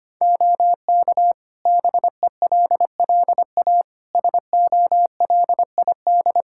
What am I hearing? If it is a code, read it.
Morse code, heard as a keyed tone, about 25 wpm, 700 Hz. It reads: OK BELLA SOLID
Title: K1USN SST QSO Practice 25wpm